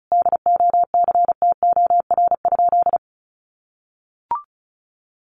Morse code, heard as a keyed tone, about 34 wpm, 700 Hz. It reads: DOCTOR?
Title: Sentences from Top 1500 Words 35wpm